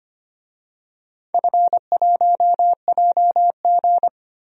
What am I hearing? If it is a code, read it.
Morse code, heard as a keyed tone, about 25 wpm, 700 Hz. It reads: F1JG
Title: DXCC-100 Call Signs - 4 Characters - 25wpm